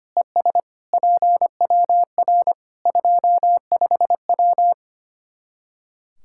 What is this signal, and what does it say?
Morse code, heard as a keyed tone, about 25 wpm, 700 Hz. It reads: ES PWR 25W
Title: K1USN SST QSO Practice 25wpm